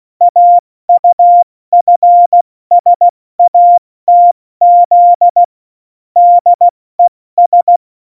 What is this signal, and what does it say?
Morse code, heard as a keyed tone, about 16 wpm, 700 Hz. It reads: AUFSATZ DES
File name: Morsekurs / Witze / Witz 19. Februar 2019, 70 F80